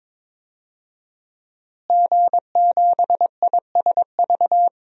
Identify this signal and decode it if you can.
Morse code, heard as a keyed tone, about 22 wpm, 700 Hz. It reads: G7ISV
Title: DXCC-100 Call Signs - 5 Characters - Repeated 2x - 22wpm